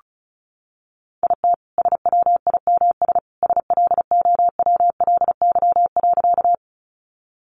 Morse code, heard as a keyed tone, about 34 wpm, 700 Hz. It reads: IT SWIMS SLOWLY.
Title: Additional Easy Phrases 35wpm